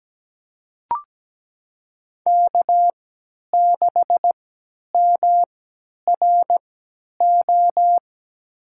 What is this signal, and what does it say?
Morse code, heard as a keyed tone, about 17 wpm, spaced as for 10 wpm, 700 Hz. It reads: K6MRO
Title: USA Call Signs - 5 Characters - 3x Character Spacing - 17wpm